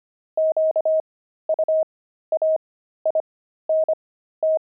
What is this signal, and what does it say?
Morse code, heard as a keyed tone, about 25 wpm, spaced as for 13 wpm, 630 Hz. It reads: QUAINT
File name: pangram-fs-13-cs-25